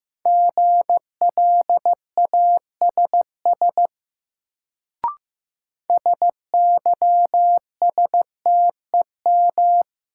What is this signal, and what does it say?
Morse code, heard as a keyed tone, about 15 wpm, 700 Hz. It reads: GLASS SYSTEM
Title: Sentences from Top 1000 Words - Review Part 3 - New Word Drill - 15wpm